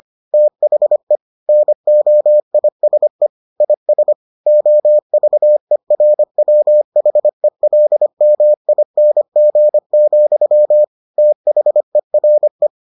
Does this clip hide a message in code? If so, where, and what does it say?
Yes: Morse code from 0.3 to 12.7 s, heard as a keyed tone, about 25 wpm, 600 Hz. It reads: THE NOISE IS OVERWHELMING, THERE